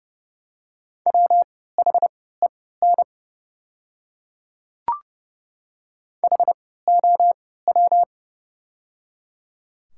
Morse code, heard as a keyed tone, about 31 wpm, spaced as for 17 wpm, 700 Hz. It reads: WHEN HOW
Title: 3x Character Spacing - Top 100 Words - 30wpm